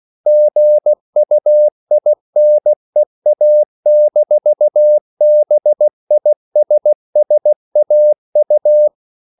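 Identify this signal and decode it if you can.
Morse code, heard as a keyed tone, about 16 wpm, 600 Hz. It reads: GUINEA-BISSAU
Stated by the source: CW practice of national capitals at 16 words per minute, from ON6ZQ.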